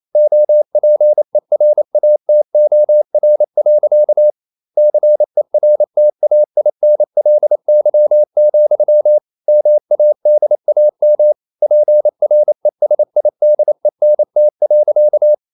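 Morse code, heard as a keyed tone, about 28 wpm, 600 Hz. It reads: OPERATOR. CERTAINLY, MADAM PRESIDENT.
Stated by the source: Morse code practice at 28 wpm, text of Ken Follett's novel Never in English, from CW med Gnister.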